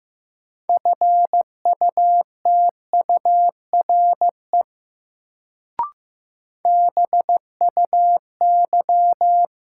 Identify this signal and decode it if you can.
Morse code, heard as a keyed tone, about 15 wpm, 700 Hz. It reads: FUTURE BUY
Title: Sentences from Top 700 Words - New Word Drill - 15wpm